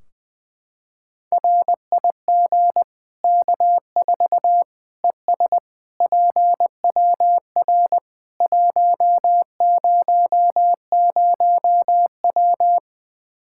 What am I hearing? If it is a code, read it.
Morse code, heard as a keyed tone, about 20 wpm, 700 Hz. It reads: RIG K4 ES PWR 100W
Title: Practice QSO - Exchange 2 - Info - 20wpm